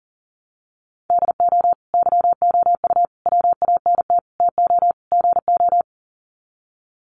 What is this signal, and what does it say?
Morse code, heard as a keyed tone, about 40 wpm, 700 Hz. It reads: DO YOU WANT TO GO
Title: Sets of 5 Words Encore2 40wpm